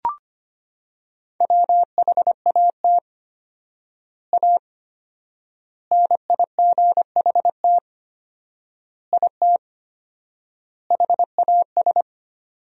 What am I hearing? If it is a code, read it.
Morse code, heard as a keyed tone, about 25 wpm, 700 Hz. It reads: WHAT A NIGHT IT HAS
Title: 4x Word-Spacing - Sentences from Top 300 Words - 25wpm